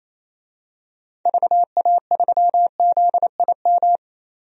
Morse code, heard as a keyed tone, about 28 wpm, 700 Hz. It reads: VA3ZIM
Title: DXCC-100 Call Signs - 6 Characters - 28wpm